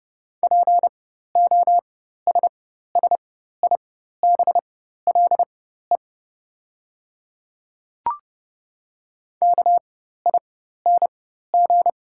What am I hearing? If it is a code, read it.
Morse code, heard as a keyed tone, about 30 wpm, spaced as for 14 wpm, 700 Hz. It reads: POSSIBLE KING
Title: 4x Character Spacing - Top 500 Words - 30wpm